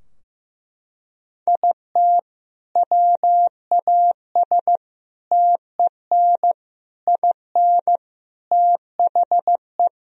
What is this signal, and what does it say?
Morse code, heard as a keyed tone, about 15 wpm, 700 Hz. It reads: IT WAS TEN IN THE
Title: Sentences from Top 1000 Words - Review Part 2 - 15wpm